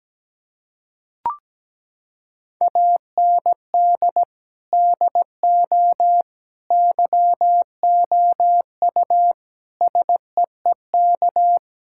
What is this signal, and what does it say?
Morse code, heard as a keyed tone, about 17 wpm, 700 Hz. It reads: AND DO YOU SEEK
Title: Sentences from Top 1600 Words 17wpm